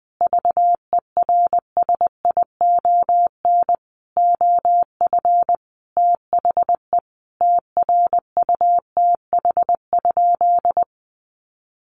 Morse code, heard as a keyed tone, about 20 wpm, 700 Hz. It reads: VERSION OF THE TRUTH?
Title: Sentences from Top 1400 Words 20wpm